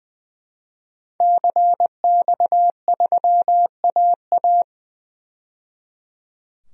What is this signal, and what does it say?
Morse code, heard as a keyed tone, about 20 wpm, 700 Hz. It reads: CX3AA
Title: DXCC-100 Call Signs - 5 Characters - 20wpm